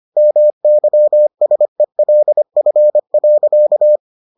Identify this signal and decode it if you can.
Morse code, heard as a keyed tone, about 25 wpm, 600 Hz. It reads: MYSELF.